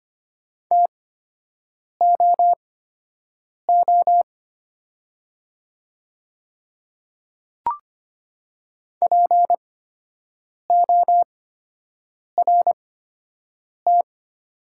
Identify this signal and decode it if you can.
Morse code, heard as a keyed tone, about 25 wpm, spaced as for 7 wpm, 690 Hz. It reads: TOO PORT
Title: 8x Character Spacing - Top 200 Words - 25wpm